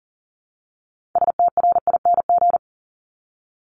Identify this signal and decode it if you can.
Morse code, heard as a keyed tone, about 41 wpm, 700 Hz. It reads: STRING
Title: Multi-letter Phonemes 40wpm